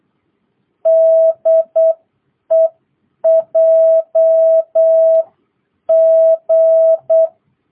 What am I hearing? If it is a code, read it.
Morse code, heard as a keyed tone, about 8 wpm, spaced as for 7 wpm, 660 Hz. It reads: DE JG